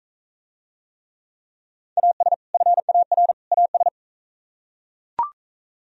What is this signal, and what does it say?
Morse code, heard as a keyed tone, about 44 wpm, 700 Hz. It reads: AS FAR AS